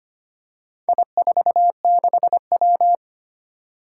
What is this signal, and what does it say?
Morse code, heard as a keyed tone, about 25 wpm, 700 Hz. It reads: I46W
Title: Four Challenging Letter-Number Combinations 25wpm